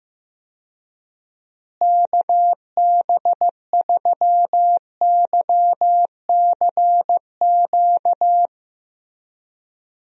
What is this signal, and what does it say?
Morse code, heard as a keyed tone, about 16 wpm, 700 Hz. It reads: KB3YCQ
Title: USA Call Signs - 6 Characters - Repeated 3x - 15wpm